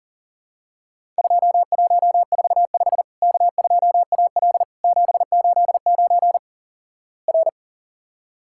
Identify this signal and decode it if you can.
Morse code, heard as a keyed tone, about 40 wpm, 700 Hz. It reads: 2145 K2AL 789 R
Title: Worked All Europe DX Contest - QTC Exchange - 40wpm